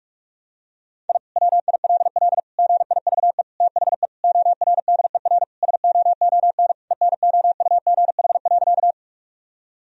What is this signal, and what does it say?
Morse code, heard as a keyed tone, about 41 wpm, 700 Hz. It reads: I WILL GIVE THE ORDER SOON ENOUGH.
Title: Sentences from Top 1000 Words - Review Part 2 - 45wpm